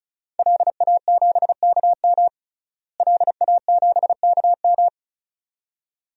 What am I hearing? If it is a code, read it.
Morse code, heard as a keyed tone, about 34 wpm, 700 Hz. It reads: LA7KM LA7KM
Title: DXCC-100 Call Signs - 5 Characters - Repeated 2x - 35wpm